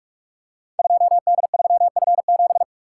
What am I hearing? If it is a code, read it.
Morse code, heard as a keyed tone, about 46 wpm, 700 Hz. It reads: 2D3F7
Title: Code Groups - 5 Letter:Numbers - 45wpm